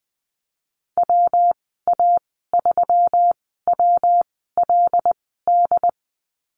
Morse code, heard as a keyed tone, about 20 wpm, spaced as for 14 wpm, 700 Hz. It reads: WA3WLD
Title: 193 Lesson 31g - Letter v - Call Signs 20wpm